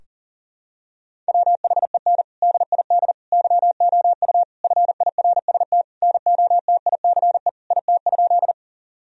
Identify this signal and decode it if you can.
Morse code, heard as a keyed tone, about 40 wpm, 700 Hz. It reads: WHEN DID YOU FIRST NOTICE IT?
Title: Sentences from Top 1000 Words - Review Part 2 - 40wpm